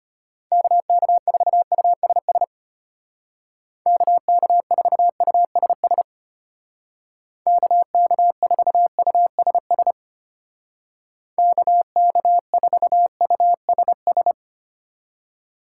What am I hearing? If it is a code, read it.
Morse code, heard as a keyed tone, about 31 wpm, 700 Hz. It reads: KK4USS KK4USS KK4USS KK4USS
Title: Speed Racing - USA 2x3 Call Signs 25wpm